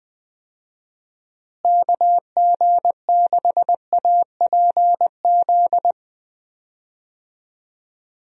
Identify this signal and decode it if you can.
Morse code, heard as a keyed tone, about 20 wpm, 700 Hz. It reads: KG6APZ
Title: USA Call Signs - 6 Characters - Repeated 3x - 20wpm